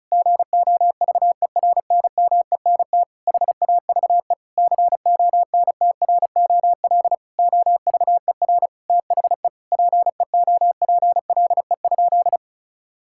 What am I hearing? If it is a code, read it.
Morse code, heard as a keyed tone, about 36 wpm, 700 Hz. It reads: GOVERNMENT HAVE CONTROL OVER THE PEOPLE?
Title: Sentences from Top 1000 Words - Review Part 5 - 35wpm